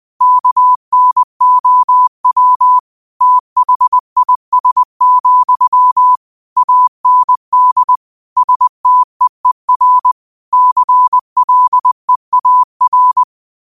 Morse code, heard as a keyed tone, about 20 wpm, 1000 Hz. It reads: KNOW THIS, AND STEER CLEAR
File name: qotdfiles / 20wpm